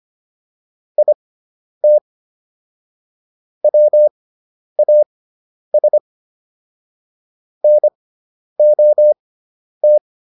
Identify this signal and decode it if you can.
Morse code, heard as a keyed tone, about 25 wpm, spaced as for 10 wpm, 600 Hz. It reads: IT WAS NOT